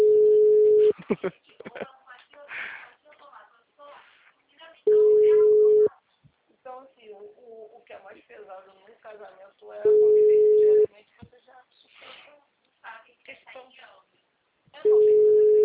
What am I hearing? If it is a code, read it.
Ringback tone (Europe, 425 Hz), with speech.